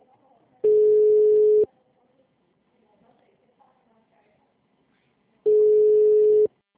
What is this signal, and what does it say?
Ringback tone (Europe, 425 Hz), with speech.